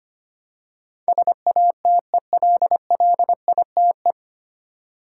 Morse code, heard as a keyed tone, about 25 wpm, 700 Hz. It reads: SATELLITE
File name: Nine Letter Words - 25wpm